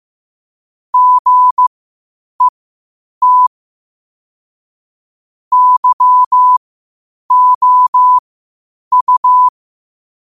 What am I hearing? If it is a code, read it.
Morse code, heard as a keyed tone, about 15 wpm, spaced as for 8 wpm, 1000 Hz. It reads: GET YOU
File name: qotdfiles / 8wpm